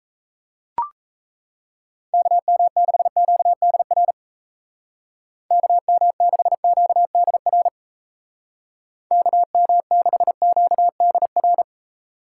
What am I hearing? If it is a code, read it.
Morse code, heard as a keyed tone, about 37 wpm, 700 Hz. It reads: KM6QDR KM6QDR KM6QDR